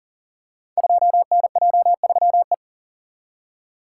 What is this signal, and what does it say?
Morse code, heard as a keyed tone, about 40 wpm, 700 Hz. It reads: 2NJ3E
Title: Code Groups - 5 Letter:Numbers - 40wpm